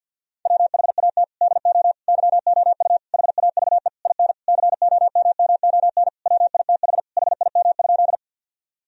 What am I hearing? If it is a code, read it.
Morse code, heard as a keyed tone, about 50 wpm, 700 Hz. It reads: WHAT DO YOU HAVE IN COMMON WITH HIM?
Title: Sentences from Top 1000 Words - Review Part 3 - 50wpm